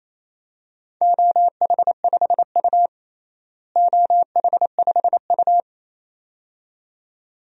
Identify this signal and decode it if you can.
Morse code, heard as a keyed tone, about 28 wpm, 700 Hz. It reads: OH5U OH5U
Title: DXCC-100 Call Signs - 4 Characters - Repeated 2x - 28wpm